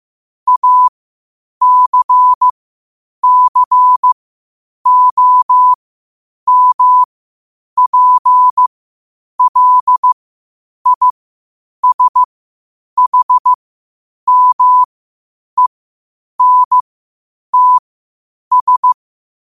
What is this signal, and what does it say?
Morse code, heard as a keyed tone, about 15 wpm, spaced as for 8 wpm, 1000 Hz. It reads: ACCOMPLISHMENTS
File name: qotdfiles / 8wpm